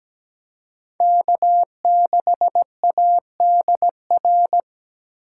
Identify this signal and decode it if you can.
Morse code, heard as a keyed tone, about 20 wpm, 700 Hz. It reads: K6ADR
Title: Speed Racing - USA 1x3 Call Signs 17wpm